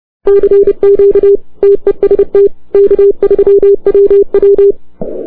Morse code, heard as a keyed tone, about 30 wpm, 400 Hz. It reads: CQ TEST K3WW
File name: arrl03cw 20 k3ww